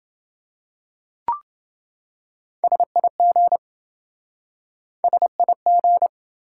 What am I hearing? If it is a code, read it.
Morse code, heard as a keyed tone, about 28 wpm, 700 Hz. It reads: SIG SIG